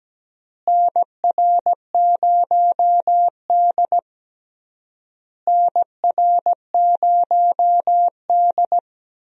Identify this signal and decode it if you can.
Morse code, heard as a keyed tone, about 17 wpm, 700 Hz. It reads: NR0D NR0D
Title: USA Call Signs - 4 Characters - Repeated 3x - 17wpm